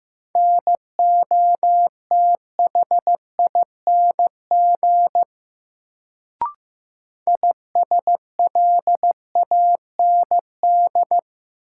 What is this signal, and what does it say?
Morse code, heard as a keyed tone, about 15 wpm, 700 Hz. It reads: NOTHING ISLAND
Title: Sentences from Top 500 Words - New Word Drill - 15wpm